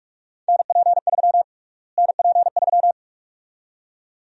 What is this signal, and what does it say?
Morse code, heard as a keyed tone, about 48 wpm, 700 Hz. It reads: NP3 NP3
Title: USA Call Sign Prefixes - Repeated 2x - 45wpm